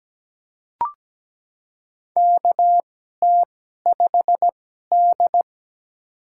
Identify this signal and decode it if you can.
Morse code, heard as a keyed tone, about 17 wpm, spaced as for 12 wpm, 700 Hz. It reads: KT5D
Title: USA Call Signs - 4 Characters - 2x Character Spacing - 17wpm